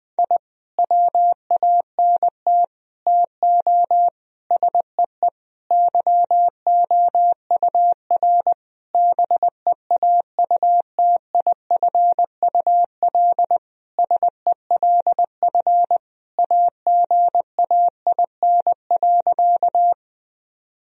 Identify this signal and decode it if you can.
Morse code, heard as a keyed tone, about 20 wpm, 700 Hz. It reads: I WANT TO SEE YOUR BEAUTIFUL SELF AGAIN.
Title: Sentences from Top 1000 Words - Review Part 1 - 20wpm